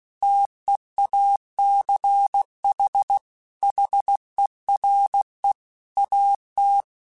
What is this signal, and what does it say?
Morse code, heard as a keyed tone, about 16 wpm, 800 Hz. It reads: TEACH HERE AT
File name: QSOs / QSOs-mp3s / WD0BGP 1@15